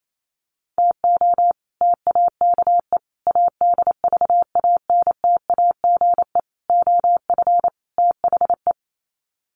Morse code, heard as a keyed tone, about 28 wpm, 700 Hz. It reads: TO TAKE ADVANTAGE OF THE
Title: Sets of 5 Words 28wpm